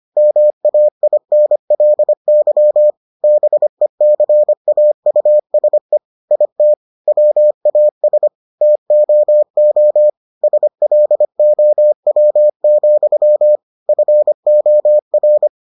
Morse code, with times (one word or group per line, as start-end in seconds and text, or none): 0.2-2.9 MAINLY
3.2-6.0 BECAUSE
6.3-6.7 IT
7.1-8.3 WAS
8.6-10.1 TOO
10.4-13.6 SLOW,
13.9-15.5 FOR